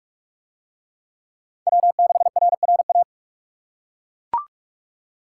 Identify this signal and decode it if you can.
Morse code, heard as a keyed tone, about 40 wpm, 700 Hz. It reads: W6RRA